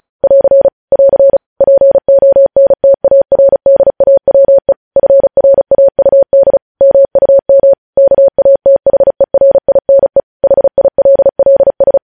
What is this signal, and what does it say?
Morse code, heard as a keyed tone, about 36 wpm, 550 Hz. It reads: <AR> <AR> PONTARDAWE FRAUD MUM KATHERINE HILLS
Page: news headlines in morse code at 35 wpm